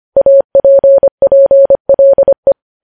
Morse code, heard as a keyed tone, about 25 wpm, 550 Hz. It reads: APPLE
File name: podcast / News headlines-1x-25x25